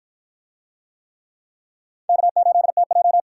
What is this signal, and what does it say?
Morse code, heard as a keyed tone, about 51 wpm, 700 Hz. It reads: K8TJ